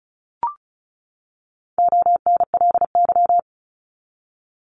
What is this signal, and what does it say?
Morse code, heard as a keyed tone, about 34 wpm, 700 Hz. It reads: ONLY
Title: Top 200 Words 35wpm